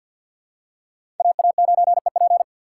Morse code, heard as a keyed tone, about 52 wpm, 700 Hz. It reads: AA9EP
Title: USA Call Signs - 5 Characters - 3x Character Spacing - 50wpm